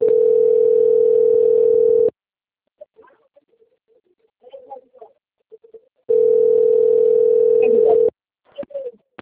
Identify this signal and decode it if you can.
Ringback tone (North America), 440 + 480 Hz.
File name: out- - - - - .2425